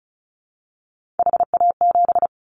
Morse code, heard as a keyed tone, about 36 wpm, 690 Hz. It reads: HA7
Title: Callsign Prefixes for Top 50 DXCC - 35wpm